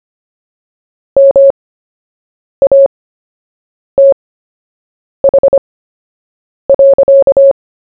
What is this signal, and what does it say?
Morse code, heard as a keyed tone, about 25 wpm, spaced as for 7 wpm, 550 Hz. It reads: MATH.
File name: cw / fists / Short-Story-1-7wpm